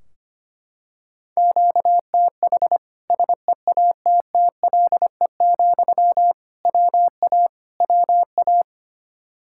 Morse code, heard as a keyed tone, about 25 wpm, 700 Hz. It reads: QTH SEATTLE, WA WA
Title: Practice QSO - Exchange 1 - Beginning - 25wpm